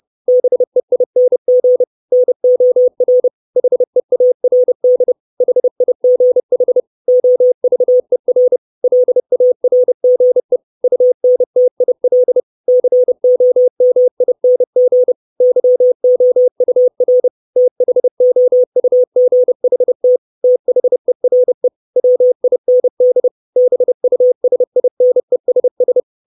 Morse code, heard as a keyed tone, about 30 wpm, 500 Hz. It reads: BEING NOR HEARD HIGH OVER LARGE UNTIL COMING YOUR THOUGHT THERE WIND BUSINESS
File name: 632 Words 30WPM